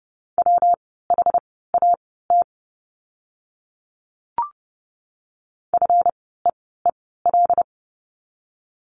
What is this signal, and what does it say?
Morse code, heard as a keyed tone, about 30 wpm, spaced as for 17 wpm, 700 Hz. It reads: WHAT FEEL
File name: course / 30:10wpm / 230 Lesson 37f - Letter x - Words 30wpm